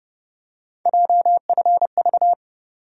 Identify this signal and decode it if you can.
Morse code, heard as a keyed tone, about 31 wpm, 700 Hz. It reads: JFV
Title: USA Call Sign Postfixes - 3 Letters 30wpm